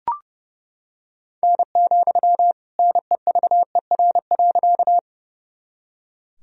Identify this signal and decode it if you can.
Morse code, heard as a keyed tone, about 29 wpm, 700 Hz. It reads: N, NEVER.
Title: Sentences from Top 300 Words and Abbreviations 30wpm